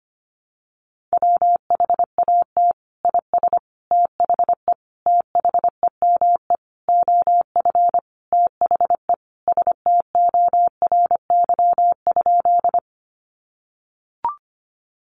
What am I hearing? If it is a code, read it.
Morse code, heard as a keyed tone, about 25 wpm, 700 Hz. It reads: WHAT IS THE THEME OF THE STORY?
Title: Sentences from Top 1100 Words 25wpm